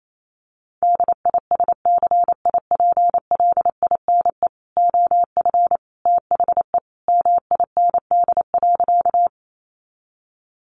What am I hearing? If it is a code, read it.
Morse code, heard as a keyed tone, about 28 wpm, 700 Hz. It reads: DISCIPLINE OF THE MIND.
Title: Sentences from Top 900 Words 28wpm